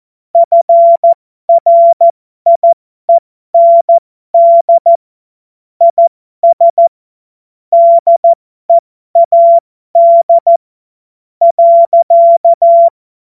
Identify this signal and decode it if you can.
Morse code, heard as a keyed tone, about 14 wpm, spaced as for 11 wpm, 680 Hz. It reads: FRIEND IS DEAD.
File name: Morsekurs / Witze